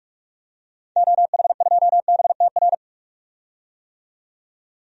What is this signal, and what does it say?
Morse code, heard as a keyed tone, about 42 wpm, 700 Hz. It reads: OH2BTR
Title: DXCC-100 Call Signs - 6 Characters - 45wpm